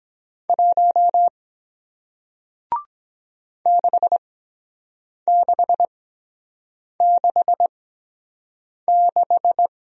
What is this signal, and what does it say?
Morse code, heard as a keyed tone, about 23 wpm, spaced as for 7 wpm, 700 Hz. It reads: 1 6666